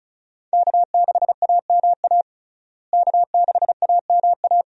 Morse code, heard as a keyed tone, about 36 wpm, 700 Hz. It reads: K6AMA K6AMA
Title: USA Call Signs - 5 Characters - Repeated 2x - 35wpm